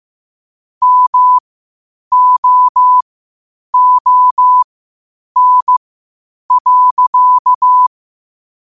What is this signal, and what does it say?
Morse code, heard as a keyed tone, about 15 wpm, spaced as for 8 wpm, 1000 Hz. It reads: MOON.
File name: qotdfiles / 8wpm